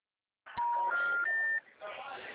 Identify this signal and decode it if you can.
Special information tone, with speech.
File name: out- -2011- - - .338